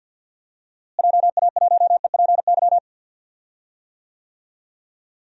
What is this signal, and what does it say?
Morse code, heard as a keyed tone, about 50 wpm, 700 Hz. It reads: JA1EPY
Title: DXCC-100 Call Signs - 6 Characters - Repeated 2x - 50wpm